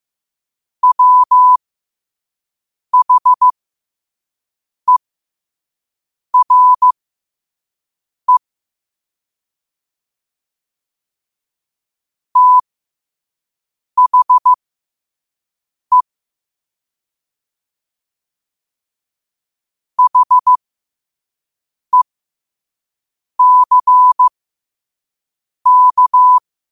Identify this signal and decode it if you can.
Morse code, heard as a keyed tone, about 15 wpm, spaced as for 5 wpm, 1000 Hz. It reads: WHERE THE HECK
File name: qotdfiles / 5wpm